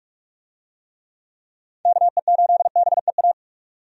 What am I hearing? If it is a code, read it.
Morse code, heard as a keyed tone, about 46 wpm, 700 Hz. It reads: KE8BEA
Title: USA Call Signs - 6 Characters - 45wpm